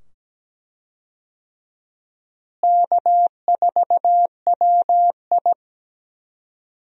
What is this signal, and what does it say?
Morse code, heard as a keyed tone, about 17 wpm, 700 Hz. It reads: K4WI
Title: USA Call Signs - 4 Characters - Repeated 2x - 17wpm